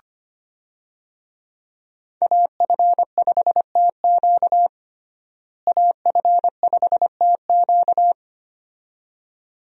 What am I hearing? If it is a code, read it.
Morse code, heard as a keyed tone, about 25 wpm, 700 Hz. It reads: AF5TQ AF5TQ
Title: USA Call Signs - 5 Characters - Repeated 2x - 25wpm